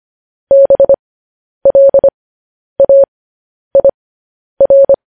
Morse code, heard as a keyed tone, about 25 wpm, spaced as for 10 wpm, 550 Hz. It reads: BLAIR
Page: news headlines in morse code at 10 wpm